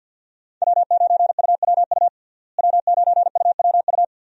Morse code, heard as a keyed tone, about 46 wpm, 700 Hz. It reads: W9UWU W9UWU
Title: USA Call Signs - 5 Characters - Repeated 2x - 50wpm